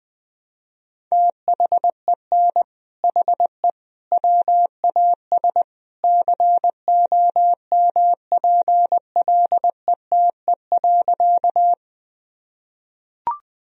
Morse code, heard as a keyed tone, about 20 wpm, 700 Hz. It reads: THEN HE WAS COMPLETE.
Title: Sentences from Top 1000 Words - Review Part 2 - 20wpm